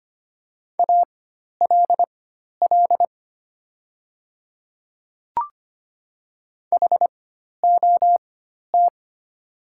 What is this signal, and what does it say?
Morse code, heard as a keyed tone, about 25 wpm, spaced as for 12 wpm, 700 Hz. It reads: ALL HOT
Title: 4x Character Spacing - Top 200 Words - 25wpm